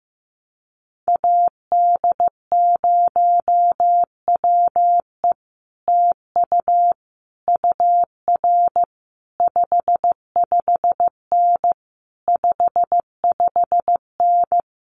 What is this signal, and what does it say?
Morse code, heard as a keyed tone, about 15 wpm, 700 Hz. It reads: AD0WE TU UR 55N 55N
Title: Common QSO Phrases Encore 15wpm